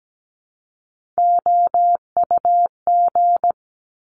Morse code, heard as a keyed tone, about 17 wpm, 700 Hz. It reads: OUG
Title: Top Three Letter Combinations in Words 17wpm